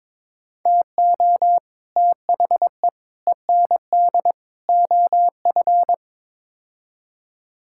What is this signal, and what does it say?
Morse code, heard as a keyed tone, about 21 wpm, 700 Hz. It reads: TO THE END OF
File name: practice / 22wpm / Build A Phrase - 5-Word Phrases - 22wpm